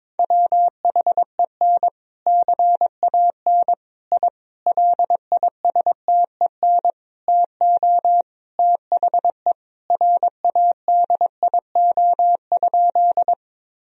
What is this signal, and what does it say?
Morse code, heard as a keyed tone, about 22 wpm, 700 Hz. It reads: WHEN CAN I LISTEN TO THE RADIO?
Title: Sentences from Top 1000 Words - Review Part 3 - 22wpm